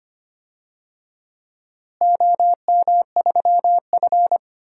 Morse code, heard as a keyed tone, about 25 wpm, 700 Hz. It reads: OM3F
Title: DXCC-100 Call Signs - 4 Characters - Repeated 2x - 25wpm